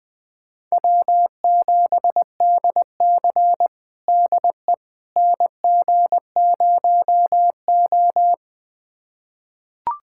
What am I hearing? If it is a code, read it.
Morse code, heard as a keyed tone, about 20 wpm, 700 Hz. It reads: W7DC DE NG0O
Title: USA Callsign DE Callsign - 4 Characters - 20wpm